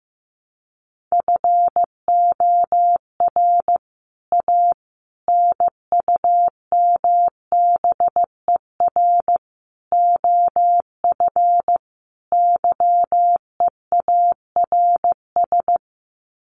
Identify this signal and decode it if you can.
Morse code, heard as a keyed tone, about 15 wpm, 700 Hz. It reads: FOR A NUMBER OF YEARS
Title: Sets of 5 Words Encore 15wpm